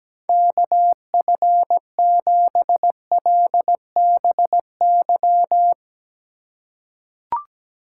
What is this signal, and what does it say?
Morse code, heard as a keyed tone, about 17 wpm, 700 Hz. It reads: KF7LBY
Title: USA Call Signs - 6 Characters - 4x Character Spacing - 17wpm